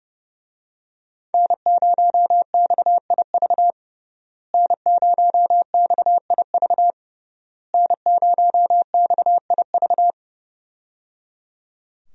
Morse code, heard as a keyed tone, about 30 wpm, 700 Hz. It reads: N0XIV N0XIV N0XIV
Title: USA Call Signs - 5 Characters - Repeated 3x - 30wpm